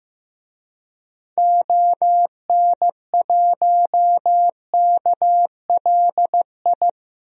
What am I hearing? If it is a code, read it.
Morse code, heard as a keyed tone, about 15 wpm, 700 Hz. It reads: ON1KLI
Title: DXCC-100 Call Signs - 6 Characters - Repeated 3x - 15wpm